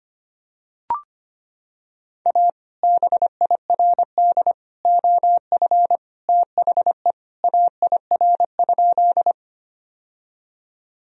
Morse code, heard as a keyed tone, about 25 wpm, 700 Hz. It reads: A BIRD OF THE AIR?
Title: Sentences from Top 1000 Words - Review Part 2 - 25wpm